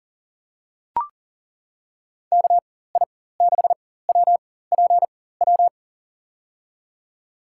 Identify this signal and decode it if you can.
Morse code, heard as a keyed tone, about 40 wpm, spaced as for 19 wpm, 700 Hz. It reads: KI6WPW
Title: USA Call Signs - 6 Characters - 4x Character Spacing - 40wpm